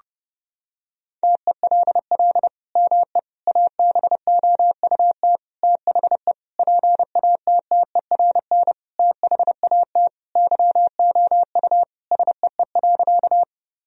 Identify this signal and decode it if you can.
Morse code, heard as a keyed tone, about 30 wpm, 700 Hz. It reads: TELL ME ABOUT THE PATTERN THAT YOU SEE.
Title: Sentences from Top 1000 Words - Review Part 2 - 30wpm